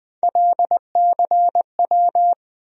Morse code, heard as a keyed tone, about 20 wpm, 700 Hz. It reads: LCW
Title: USA Call Sign Postfixes - 3 Letters 20wpm